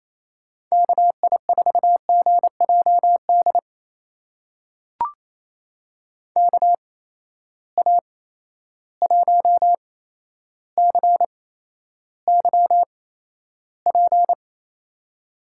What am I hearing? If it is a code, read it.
Morse code, heard as a keyed tone, about 28 wpm, spaced as for 12 wpm, 700 Hz. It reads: KI4GJD KA1CYP
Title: USA Call Signs - 6 Characters - 8x Character Spacing - 28wpm